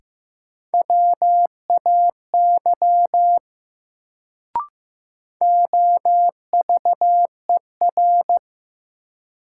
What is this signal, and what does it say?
Morse code, heard as a keyed tone, about 15 wpm, 700 Hz. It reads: WAY OVER
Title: Sets of 2 Words - New Word Drill - 15wpm